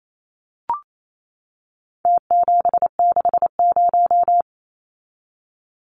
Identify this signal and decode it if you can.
Morse code, heard as a keyed tone, about 28 wpm, 700 Hz. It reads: T760
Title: Numbers - Contest Style - 4-Digit 28wpm